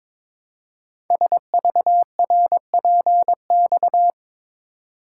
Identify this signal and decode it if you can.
Morse code, heard as a keyed tone, about 23 wpm, 700 Hz. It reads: SVRPX
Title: Code Groups - 5 Letters - 22wpm